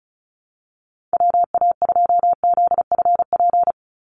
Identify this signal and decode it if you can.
Morse code, heard as a keyed tone, about 34 wpm, 700 Hz. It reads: WA2ZFP
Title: USA Call Signs 35wpm